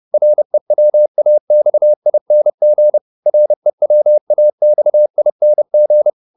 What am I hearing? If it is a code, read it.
Morse code, heard as a keyed tone, about 30 wpm, 600 Hz. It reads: REWAXING REWAXING